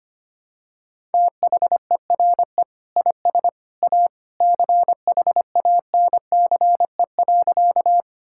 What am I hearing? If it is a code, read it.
Morse code, heard as a keyed tone, about 25 wpm, 700 Hz. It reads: THERE IS A CHANCE.
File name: practice / 25wpm / Sentences with 2-Word Phrases - 25wpm